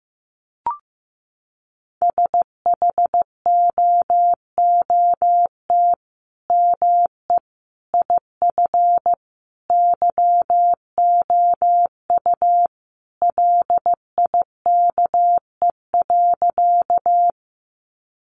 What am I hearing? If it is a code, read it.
Morse code, heard as a keyed tone, about 15 wpm, 700 Hz. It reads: SHOOT ME IF YOU LIKE.
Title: Sentences from Top 1200 Words 15wpm